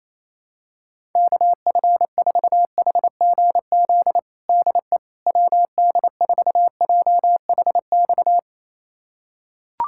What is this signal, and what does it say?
Morse code, heard as a keyed tone, about 28 wpm, 700 Hz. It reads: KF4HGZ DE WD4JHX
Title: USA Callsign DE Callsign - 6 Characters - 28wpm